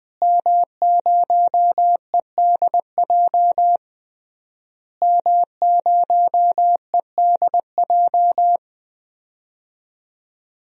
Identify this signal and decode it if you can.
Morse code, heard as a keyed tone, about 20 wpm, 700 Hz. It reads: M0EDJ M0EDJ
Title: DXCC-100 Call Signs - 5 Characters - Repeated 3x - 20wpm